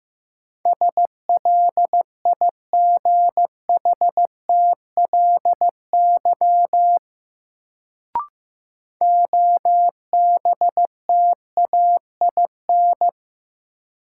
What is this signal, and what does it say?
Morse code, heard as a keyed tone, about 16 wpm, 700 Hz. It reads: SLIGHTLY OBTAIN
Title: Sentences from Top 1300 Words - New Word Drill - 15wpm